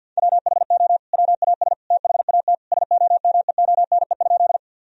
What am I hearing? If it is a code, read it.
Morse code, heard as a keyed tone, about 52 wpm, 700 Hz. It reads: WHO WAS THAT SOMEONE?
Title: Sentences from Top 1000 Words - Review Part 4 - 50wpm